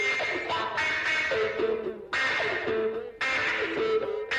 Emergency Alert System (SAME) tones, heard as harsh buzzing bursts. It is heard from 0.8 to 3.5 s.
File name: wrct-log hi- 18